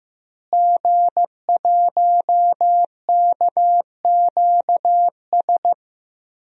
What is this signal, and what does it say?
Morse code, heard as a keyed tone, about 15 wpm, 700 Hz. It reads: G1KQS
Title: DXCC-100 Call Signs - 5 Characters - Repeated 2x - 15wpm